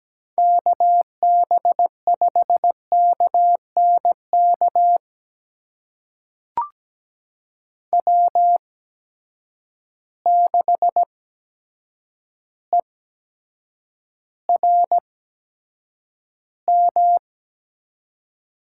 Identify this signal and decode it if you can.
Morse code, heard as a keyed tone, about 17 wpm, spaced as for 8 wpm, 700 Hz. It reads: KB5KNK W6ERM
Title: USA Call Signs - 8x Character Spacing - 17wpm